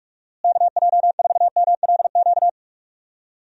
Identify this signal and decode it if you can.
Morse code, heard as a keyed tone, about 41 wpm, 700 Hz. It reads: KJ4MLQ
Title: USA Call Signs - 6 Characters - Repeated 3x - 45wpm